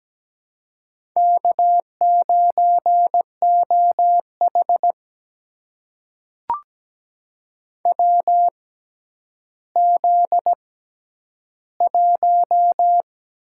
Morse code, heard as a keyed tone, about 17 wpm, spaced as for 10 wpm, 700 Hz. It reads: K9OH WZ1
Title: USA Call Signs - 4 Characters - 6x Character Spacing - 17wpm